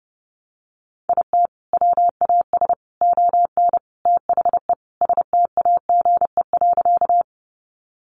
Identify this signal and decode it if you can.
Morse code, heard as a keyed tone, about 30 wpm, 700 Hz. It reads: IT WAS ON THE STAGE.
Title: Sentences from Top 1200 Words 30wpm